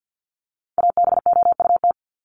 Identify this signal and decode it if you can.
Morse code, heard as a keyed tone, about 50 wpm, 700 Hz. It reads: ABOUT
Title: Top 500 Words 50wpm